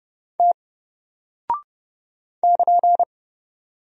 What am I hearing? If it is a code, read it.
Morse code, heard as a keyed tone, about 30 wpm, 700 Hz. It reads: T <KN>
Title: Single Character (Letter:Number:Punctuation:Prosign) 30wpm